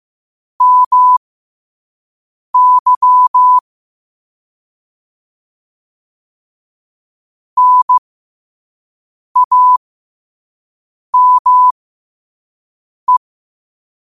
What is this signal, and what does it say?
Morse code, heard as a keyed tone, about 15 wpm, spaced as for 5 wpm, 1000 Hz. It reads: MY NAME
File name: qotdfiles / 5wpm